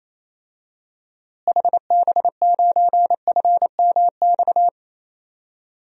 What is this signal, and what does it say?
Morse code, heard as a keyed tone, about 28 wpm, 700 Hz. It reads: HB9FMX
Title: DXCC-100 Call Signs - 6 Characters - Repeated 2x - 28wpm